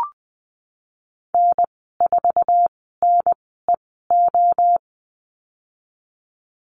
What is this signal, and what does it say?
Morse code, heard as a keyed tone, about 20 wpm, spaced as for 14 wpm, 700 Hz. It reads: N4NEO
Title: 069 Lesson 12g - Letter d - Call Signs 20wpm